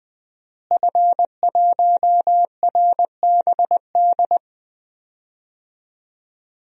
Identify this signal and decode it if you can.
Morse code, heard as a keyed tone, about 20 wpm, 700 Hz. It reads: F1RBD
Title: DXCC-100 Call Signs - 5 Characters - Repeated 2x - 20wpm